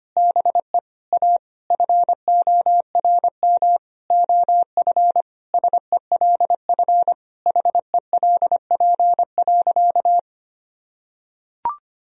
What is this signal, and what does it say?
Morse code, heard as a keyed tone, about 25 wpm, 700 Hz. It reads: BE A FORM OF SELF HELP.
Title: Sentences from Top 1000 Words - Review Part 1 - 25wpm